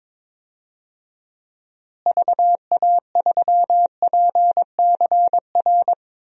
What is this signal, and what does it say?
Morse code, heard as a keyed tone, about 22 wpm, 700 Hz. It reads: VA3PCR
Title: DXCC-100 Call Signs - 6 Characters - Repeated 3x - 22wpm